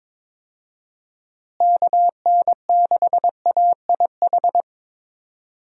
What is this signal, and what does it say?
Morse code, heard as a keyed tone, about 22 wpm, 700 Hz. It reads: KN6AIH
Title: USA Call Signs - 6 Characters - Repeated 2x - 22wpm